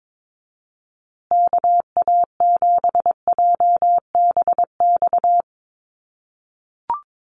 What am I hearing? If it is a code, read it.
Morse code, heard as a keyed tone, about 22 wpm, 700 Hz. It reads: KA7JBX
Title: USA Call Signs Repeated 3x 22wpm